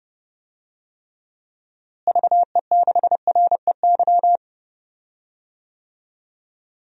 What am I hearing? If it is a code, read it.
Morse code, heard as a keyed tone, about 30 wpm, 700 Hz. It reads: VE6REY
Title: DXCC-100 Call Signs - 6 Characters - Repeated 3x - 30wpm